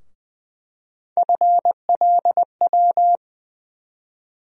Morse code, heard as a keyed tone, about 20 wpm, 700 Hz. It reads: FLW
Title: USA Call Sign Postfixes - 3 Letters 20wpm